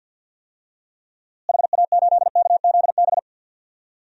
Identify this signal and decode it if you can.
Morse code, heard as a keyed tone, about 52 wpm, 700 Hz. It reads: HA8KZB
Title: DXCC-100 Call Signs - 6 Characters - Repeated 3x - 50wpm